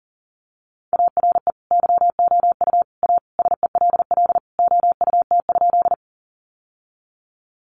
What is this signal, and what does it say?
Morse code, heard as a keyed tone, about 40 wpm, 700 Hz. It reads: ARE YOU A SELL OUT?
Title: Sentences from Top 1100 Words 40wpm